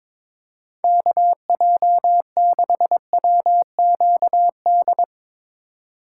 Morse code, heard as a keyed tone, about 21 wpm, 700 Hz. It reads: KJ6WQD
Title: USA Call Signs - 6 Characters - 3x Character Spacing - 22wpm